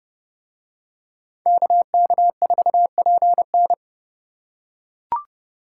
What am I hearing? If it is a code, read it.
Morse code, heard as a keyed tone, about 27 wpm, 700 Hz. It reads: KK4PN